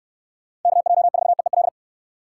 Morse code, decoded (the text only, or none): KJ4EEL